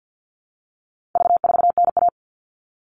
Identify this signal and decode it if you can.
Morse code, heard as a keyed tone, about 48 wpm, 700 Hz. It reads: V4NA
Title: All Cut Numbers - 4-Digit 50wpm